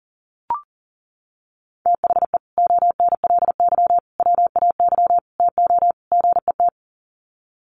Morse code, heard as a keyed tone, about 40 wpm, 700 Hz. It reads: THE ONLY WAY TO GET
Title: Sets of 5 Words Encore2 40wpm